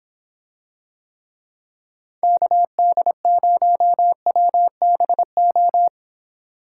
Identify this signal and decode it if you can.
Morse code, heard as a keyed tone, about 26 wpm, 700 Hz. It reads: KD0WBO